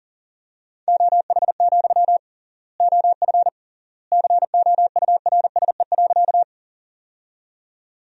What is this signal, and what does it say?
Morse code, heard as a keyed tone, about 40 wpm, 700 Hz. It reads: OH, OF COURSE.
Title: 3x Word-Spacing - Sentences from Top 500 Words - 40wpm